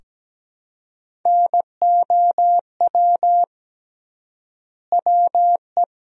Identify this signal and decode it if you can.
Morse code, heard as a keyed tone, about 17 wpm, 700 Hz. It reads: NOW WE
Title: 3x Word-Spacing - Sentences from Top 500 Words - 17wpm